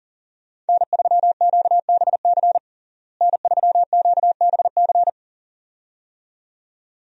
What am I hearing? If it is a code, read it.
Morse code, heard as a keyed tone, about 40 wpm, 700 Hz. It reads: N3QBC N3QBC
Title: USA Call Signs - 5 Characters - Repeated 2x - 40wpm